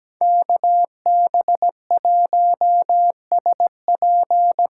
Morse code, heard as a keyed tone, about 17 wpm, 700 Hz. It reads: KB1SP
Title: USA Call Signs - 6 Characters - 6x Character Spacing - 17wpm